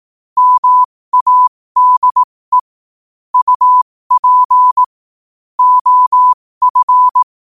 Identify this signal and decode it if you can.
Morse code, heard as a keyed tone, about 18 wpm, spaced as for 15 wpm, 1000 Hz. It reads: MADE UP OF
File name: qotdfiles / 15wpm